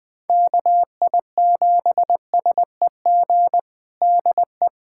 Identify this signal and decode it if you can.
Morse code, heard as a keyed tone, about 21 wpm, 700 Hz. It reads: KI7SEG DE
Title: USA Callsign DE Callsign - 6 Characters - 20wpm